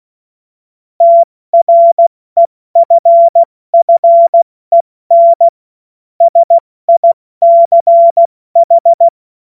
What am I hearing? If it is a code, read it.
Morse code, heard as a keyed tone, about 16 wpm, 680 Hz. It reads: TREFFEN SICH
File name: Morsekurs / Witze / Witz 26. April 2022 70 F80